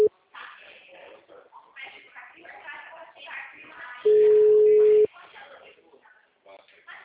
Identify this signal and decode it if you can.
Ringback tone (Europe, 425 Hz), with speech.